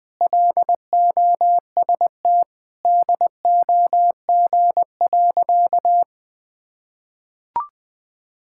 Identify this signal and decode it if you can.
Morse code, heard as a keyed tone, about 20 wpm, 700 Hz. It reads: LOST DOG.
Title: Sentences with 2-Word Phrases - 20wpm